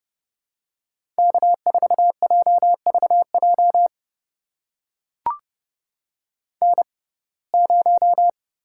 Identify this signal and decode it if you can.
Morse code, heard as a keyed tone, about 30 wpm, spaced as for 22 wpm, 700 Hz. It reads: K4JVJ N0
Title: USA Call Signs - 5 Characters - 6x Character Spacing - 30wpm